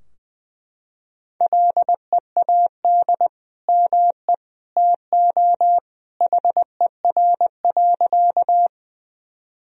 Morse code, heard as a keyed tone, about 20 wpm, 700 Hz. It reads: LEAD ME TO HER.
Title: Sentences from Top 1000 Words - Review Part 2 - 20wpm